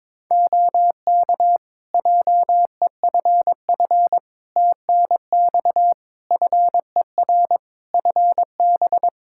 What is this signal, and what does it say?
Morse code, heard as a keyed tone, about 22 wpm, 700 Hz. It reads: OK JEFF TNX FER FB
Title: Practice QSO - Exchange 4 - Ending - 22wpm